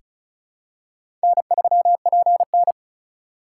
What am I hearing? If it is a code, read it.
Morse code, heard as a keyed tone, about 36 wpm, 700 Hz. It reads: N3PN
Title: USA Call Signs - 3x Character Spacing - 35wpm